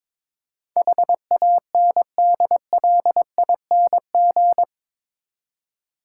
Morse code, heard as a keyed tone, about 22 wpm, 700 Hz. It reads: HANDLING
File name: practice / 22wpm / Eight Letter Words - 22wpm